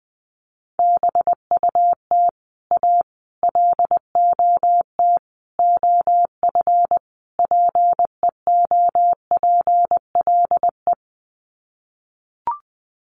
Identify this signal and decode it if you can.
Morse code, heard as a keyed tone, about 20 wpm, 700 Hz. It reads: BUT A LOT OF PEOPLE
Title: Sets of 5 Words Encore2 20wpm